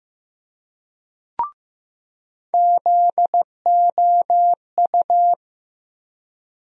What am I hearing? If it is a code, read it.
Morse code, heard as a keyed tone, about 15 wpm, 700 Hz. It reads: ZOU
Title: USA Call Sign Postfixes - 3 Letters 15wpm